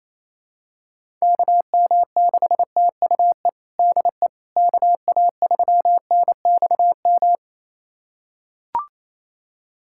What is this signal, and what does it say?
Morse code, heard as a keyed tone, about 27 wpm, 700 Hz. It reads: KM6TUE DE KA3NXM
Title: USA Callsign DE Callsign 28wpm